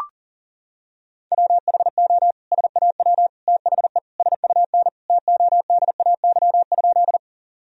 Morse code, heard as a keyed tone, about 40 wpm, 700 Hz. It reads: WHO SAW THE SUN TODAY?
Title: Sentences from Top 1000 Words - Review Part 1 - 40wpm